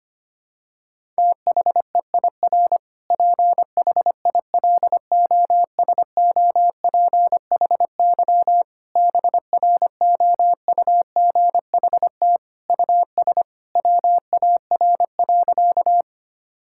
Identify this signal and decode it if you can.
Morse code, heard as a keyed tone, about 25 wpm, 700 Hz. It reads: THEIR PHILOSOPHY BROUGHT US WAR.
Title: Sentences from Top 1000 Words - Review Part 6 - 25wpm